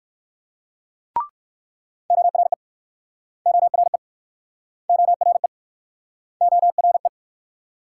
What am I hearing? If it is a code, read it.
Morse code, heard as a keyed tone, about 61 wpm, 690 Hz. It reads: ORE ORE ORE ORE